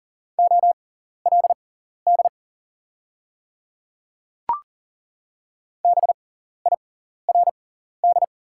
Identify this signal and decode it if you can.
Morse code, heard as a keyed tone, about 39 wpm, spaced as for 14 wpm, 690 Hz. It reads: OLD BIRD
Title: 6x Character Spacing - Top 500 Words - 40wpm